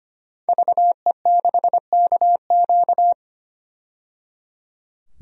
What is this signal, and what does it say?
Morse code, heard as a keyed tone, about 25 wpm, 700 Hz. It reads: VE6KQ
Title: DXCC-100 Call Signs - 5 Characters - 25wpm